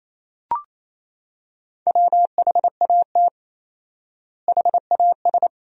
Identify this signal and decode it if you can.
Morse code, heard as a keyed tone, about 28 wpm, 700 Hz. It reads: WHAT HAS
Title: 4x Word-Spacing - Sentences from Top 500 Words - 28wpm